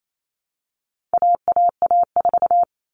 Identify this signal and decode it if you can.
Morse code, heard as a keyed tone, about 28 wpm, 700 Hz. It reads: AAA4
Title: All Cut Numbers - 4-Digit 28wpm